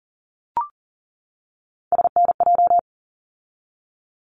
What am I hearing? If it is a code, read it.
Morse code, heard as a keyed tone, about 40 wpm, 700 Hz. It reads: SNJ
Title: ARRL Sections 40wpm